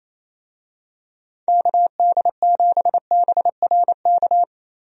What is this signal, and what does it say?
Morse code, heard as a keyed tone, about 28 wpm, 700 Hz. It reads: KD7BRK
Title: USA Call Signs - 6 Characters - Repeated 3x - 28wpm